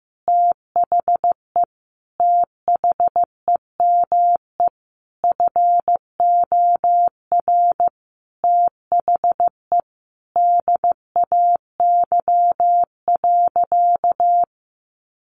Morse code, heard as a keyed tone, about 14 wpm, 700 Hz. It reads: THE THEME FOR THE DAY.
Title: Sentences from Top 1100 Words 15wpm